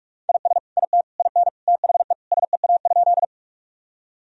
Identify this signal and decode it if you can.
Morse code, heard as a keyed tone, about 43 wpm, 700 Hz. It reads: IS IT IN THE SEA?
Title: Sentences from Top 1000 Words - Review Part 1 - 45wpm